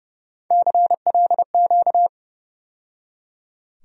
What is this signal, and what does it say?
Morse code, heard as a keyed tone, about 30 wpm, 700 Hz. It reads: CLQ
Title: USA Call Sign Postfixes - 3 Letters 30wpm